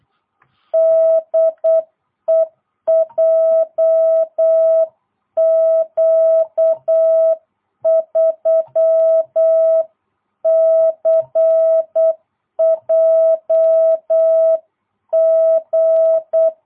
Morse code, heard as a keyed tone, about 8 wpm, 660 Hz. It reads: DE JQ3CJG